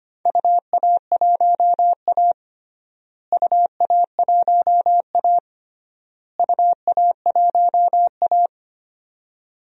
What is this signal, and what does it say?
Morse code, heard as a keyed tone, about 25 wpm, 700 Hz. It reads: UA1A UA1A UA1A
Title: DXCC-100 Call Signs - 4 Characters - Repeated 3x - 25wpm